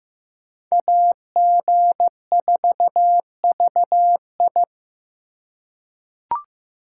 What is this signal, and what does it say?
Morse code, heard as a keyed tone, about 15 wpm, 700 Hz. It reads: AG4VI
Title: USA Call Signs - 5 Characters - 3x Character Spacing - 15wpm